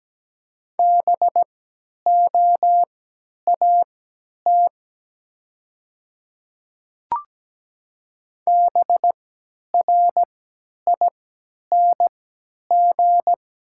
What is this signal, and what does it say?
Morse code, heard as a keyed tone, about 17 wpm, spaced as for 10 wpm, 700 Hz. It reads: BOAT BRING
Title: 3x Character Spacing - Top 500 Words - 17wpm